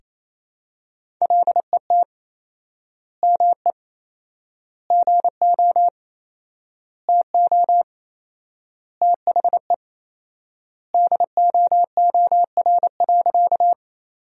Morse code, heard as a keyed tone, about 28 wpm, 700 Hz. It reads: LET ME GO TO THE DOOR.
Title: 4x Word-Spacing - Sentences from Top 300 Words - 28wpm